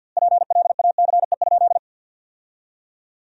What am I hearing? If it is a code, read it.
Morse code, heard as a keyed tone, about 50 wpm, 700 Hz. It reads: PLACE?
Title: 2x Word-Spacing - Sentences from Top 200 Words - 50wpm